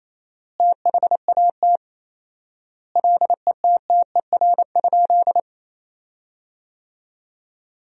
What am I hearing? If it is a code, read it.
Morse code, heard as a keyed tone, about 28 wpm, 700 Hz. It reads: THAT LETTER?
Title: 4x Word-Spacing - Sentences from Top 300 Words - 28wpm